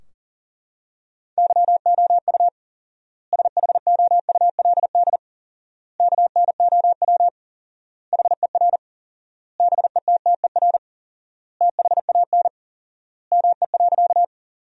Morse code, heard as a keyed tone, about 40 wpm, 700 Hz. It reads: YOU SHOULD KNOW HER BETTER THAN ME.
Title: 4x Word-Spacing - Sentences from Top 400 Words - 40wpm